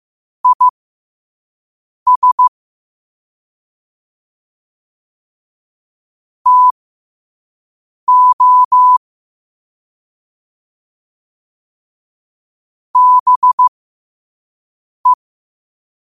Morse code, heard as a keyed tone, about 15 wpm, spaced as for 5 wpm, 1000 Hz. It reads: IS TO BE
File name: qotdfiles / 5wpm